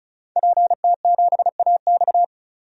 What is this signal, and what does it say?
Morse code, heard as a keyed tone, about 35 wpm, 700 Hz. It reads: PT7AX
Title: DXCC-100 Call Signs - 5 Characters - 35wpm